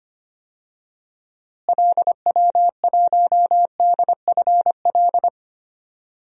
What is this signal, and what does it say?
Morse code, heard as a keyed tone, about 25 wpm, 700 Hz. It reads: LW1DFL
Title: DXCC-100 Call Signs - 6 Characters - Repeated 3x - 25wpm